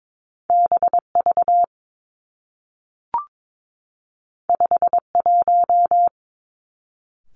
Morse code, heard as a keyed tone, about 22 wpm, 700 Hz. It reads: B V 5 1
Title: Two Challenging Letter-Number Combinations 22wpm